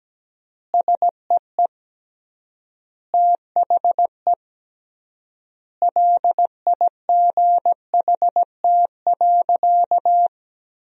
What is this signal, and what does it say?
Morse code, heard as a keyed tone, about 17 wpm, 700 Hz. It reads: SEE THE LIGHT.
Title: 3x Word-Spacing - Sentences from Top 200 Words - 17wpm